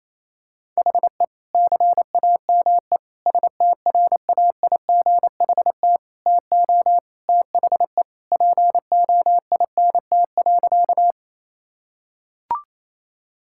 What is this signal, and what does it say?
Morse code, heard as a keyed tone, about 28 wpm, 700 Hz. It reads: HE CAME STRAIGHT TO THE POINT.
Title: Sentences from Top 1000 Words - Review Part 6 - 28wpm